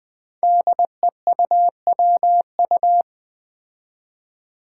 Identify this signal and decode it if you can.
Morse code, heard as a keyed tone, about 20 wpm, 700 Hz. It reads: DEUWU
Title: Five Challenging Letter-Number Combinations 20wpm